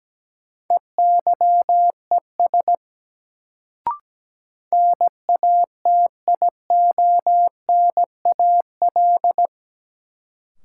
Morse code, heard as a keyed tone, about 17 wpm, 700 Hz. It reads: EYES NATIONAL
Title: Sentences from Top 1000 Words - Review Part 4 - New Word Drill - 17wpm